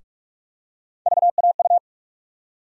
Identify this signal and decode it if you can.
Morse code, heard as a keyed tone, about 46 wpm, 700 Hz. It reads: VAU
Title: USA Call Sign Postfixes - 3 Letters 45wpm